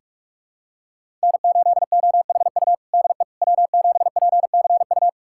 Morse code, heard as a keyed tone, about 46 wpm, 700 Hz. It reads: N8OHU DE W7PCU
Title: USA Callsign DE Callsign - 5 Characters - 45wpm